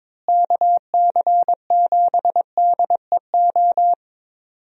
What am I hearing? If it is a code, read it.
Morse code, heard as a keyed tone, about 22 wpm, 700 Hz. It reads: KC7DEO
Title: USA Call Signs - 6 Characters - Repeated 2x - 22wpm